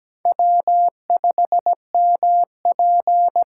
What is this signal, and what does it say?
Morse code, heard as a keyed tone, about 17 wpm, 700 Hz. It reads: W5MP
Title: USA Call Signs - 4 Characters - 6x Character Spacing - 17wpm